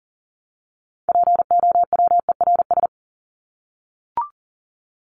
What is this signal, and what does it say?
Morse code, heard as a keyed tone, about 40 wpm, 700 Hz. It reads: POWERS
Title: ES:S Suffix - Top 100 Words with Base Words 40wpm